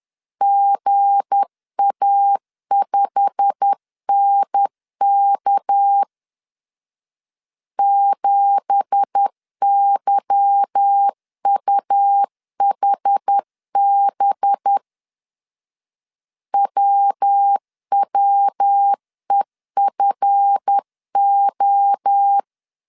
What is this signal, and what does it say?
Morse code, heard as a keyed tone, about 10 wpm, 800 Hz. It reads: GA5NK 7YUHB WWEFO
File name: CW / DET / 1 9 1500C S A 10wpm 4511